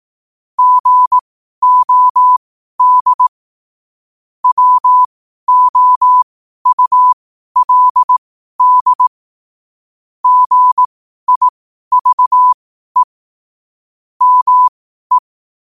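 Morse code, heard as a keyed tone, about 18 wpm, spaced as for 12 wpm, 1000 Hz. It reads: GOD WOULD GIVE ME
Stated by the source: Morse code sent at 12 words per minute.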